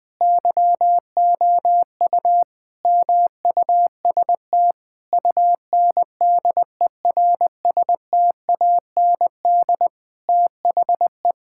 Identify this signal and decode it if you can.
Morse code, heard as a keyed tone, about 20 wpm, 700 Hz. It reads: YOU MUST UNDERSTAND THE
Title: Sentences from Top 1000 Words - Review Part 5 - 20wpm